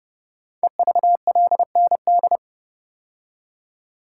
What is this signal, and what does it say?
Morse code, heard as a keyed tone, about 30 wpm, 700 Hz. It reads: EVLND
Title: Code Groups - 5 Letter:Numbers - 30wpm